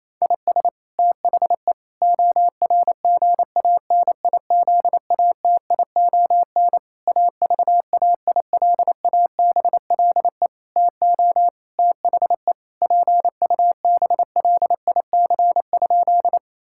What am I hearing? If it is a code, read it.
Morse code, heard as a keyed tone, about 28 wpm, 700 Hz. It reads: IS THE ORGANIZATION AVAILABLE TO THE PUBLIC?
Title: Sentences from Top 1000 Words - Review Part 5 - 28wpm